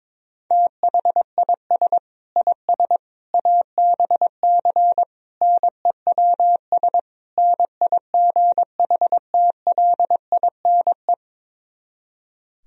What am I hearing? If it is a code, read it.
Morse code, heard as a keyed tone, about 22 wpm, 700 Hz. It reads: THIS IS ABC NEWS NIGHTLINE
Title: Speed-Racing - Sets of 5 Words 22wpm